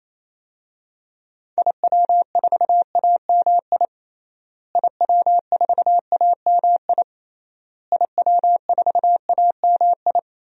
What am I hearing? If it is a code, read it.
Morse code, heard as a keyed tone, about 28 wpm, 700 Hz. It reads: IW4AMI IW4AMI IW4AMI
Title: DXCC-100 Call Signs - 6 Characters - Repeated 3x - 28wpm